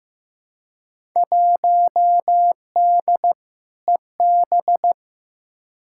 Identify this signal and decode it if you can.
Morse code, heard as a keyed tone, about 15 wpm, 700 Hz. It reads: 1D EB
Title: Field Day Exchanges - 15wpm